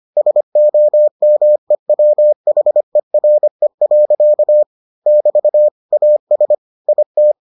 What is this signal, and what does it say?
Morse code, heard as a keyed tone, about 25 wpm, 600 Hz. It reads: SOMEWHERE. <BT> AS IT